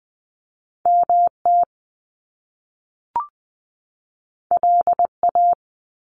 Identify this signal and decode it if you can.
Morse code, heard as a keyed tone, about 20 wpm, 700 Hz. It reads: MT LA
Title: US States 20wpm